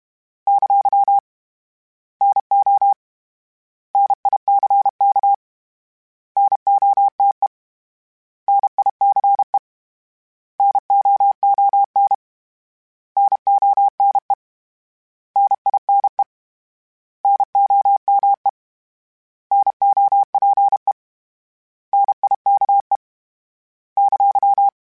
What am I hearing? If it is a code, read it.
Morse code, heard as a keyed tone, about 32 wpm, 800 Hz. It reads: ! NO NICK NOTE NICE NOON NONE NINE NOME NOPE NIKE !
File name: Misc / Misc-mp3s / Misc @30